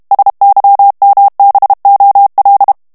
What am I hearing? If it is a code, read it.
Morse code, heard as a keyed tone, about 32 wpm, 800 Hz. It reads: SYMBOL
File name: ANGLAIS texte clair 3 vitesse 30 mots minute